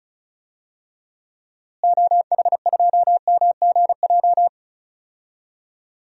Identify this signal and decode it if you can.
Morse code, heard as a keyed tone, about 36 wpm, 700 Hz. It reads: OH2MGJ
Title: DXCC-100 Call Signs - 6 Characters - Repeated 3x - 35wpm